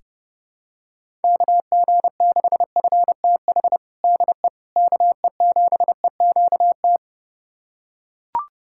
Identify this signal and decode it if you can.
Morse code, heard as a keyed tone, about 30 wpm, 700 Hz. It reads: KG6FTH DE KE7EQT
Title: USA Callsign DE Callsign - 6 Characters - 30wpm